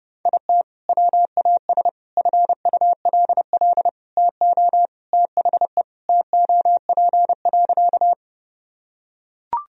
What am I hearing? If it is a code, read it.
Morse code, heard as a keyed tone, about 30 wpm, 700 Hz. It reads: IT WAS FULL TO THE TOP.
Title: Sentences from Top 1000 Words - Review Part 3 - 30wpm